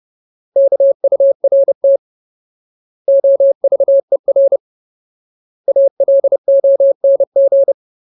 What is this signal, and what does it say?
Morse code, heard as a keyed tone, about 30 wpm, 550 Hz. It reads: KURT OVER ALONG
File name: Ultimate Word List / 30WPM Ultimate